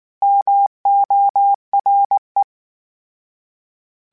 Morse code, heard as a keyed tone, about 19 wpm, 800 Hz. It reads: MORE